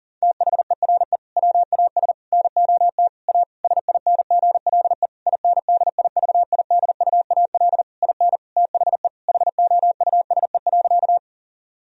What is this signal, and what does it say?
Morse code, heard as a keyed tone, about 40 wpm, 700 Hz. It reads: THERE WAS NOT A SINGLE INDIVIDUAL IN THE HOUSE.
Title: Sentences from Top 1000 Words - Review Part 4 - 40wpm